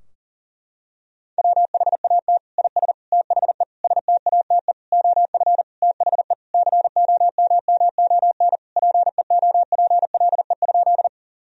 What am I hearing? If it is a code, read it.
Morse code, heard as a keyed tone, about 40 wpm, 700 Hz. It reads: WHAT IS THE STATE OF THE COMMON PEOPLE?
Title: Sentences from Top 1000 Words - Review Part 3 - 40wpm